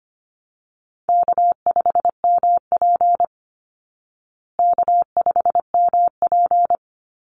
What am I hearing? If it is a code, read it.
Morse code, heard as a keyed tone, about 25 wpm, 700 Hz. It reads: K5MP K5MP
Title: Easy Call Signs Repeated 3x - 25wpm